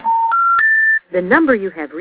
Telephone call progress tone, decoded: special information tone